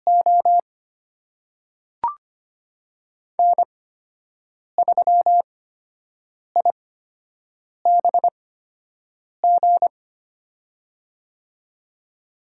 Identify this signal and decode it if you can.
Morse code, heard as a keyed tone, about 25 wpm, spaced as for 7 wpm, 700 Hz. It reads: O N3IBG
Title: USA Call Signs - 5 Characters - 8x Character Spacing - 25wpm